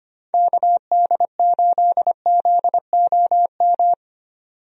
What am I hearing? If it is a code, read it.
Morse code, heard as a keyed tone, about 25 wpm, 700 Hz. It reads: KD8ZOM
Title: USA Call Signs - 6 Characters - 25wpm